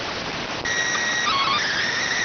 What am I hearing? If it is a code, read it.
SSTV (slow-scan television), PD 290.